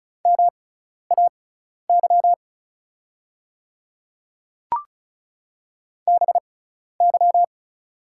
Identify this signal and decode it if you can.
Morse code, heard as a keyed tone, about 36 wpm, spaced as for 12 wpm, 690 Hz. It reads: MAY BY
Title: 6x Character Spacing - Top 100 Words - 35wpm